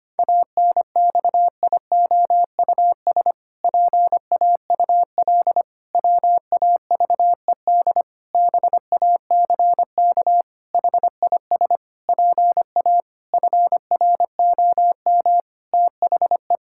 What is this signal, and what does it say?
Morse code, heard as a keyed tone, about 25 wpm, 700 Hz. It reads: ANXIOUS PAUL WAVED BACK HIS PA FROM THE
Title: Pangram Challenge v2 - 25wpm